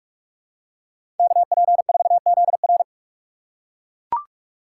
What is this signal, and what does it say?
Morse code, heard as a keyed tone, about 42 wpm, 700 Hz. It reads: KP4ZR
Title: USA Call Signs - 5 Characters - 2x Character Spacing - 45wpm